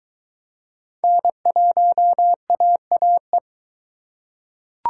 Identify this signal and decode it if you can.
Morse code, heard as a keyed tone, about 22 wpm, 700 Hz. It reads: N1AAE